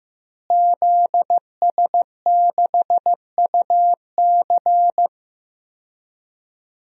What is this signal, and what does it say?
Morse code, heard as a keyed tone, about 15 wpm, 700 Hz. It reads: ZS6UC
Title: DXCC-100 Call Signs - 5 Characters - Repeated 2x - 15wpm